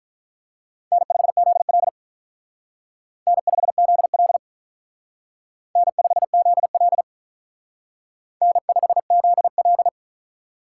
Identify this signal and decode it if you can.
Morse code, heard as a keyed tone, about 45 wpm, 700 Hz. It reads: N5ZL N5ZL N5ZL N5ZL